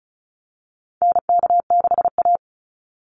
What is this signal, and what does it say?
Morse code, heard as a keyed tone, about 34 wpm, 700 Hz. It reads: NK6A
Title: NA SOTA Chaser Callsigns 35wpm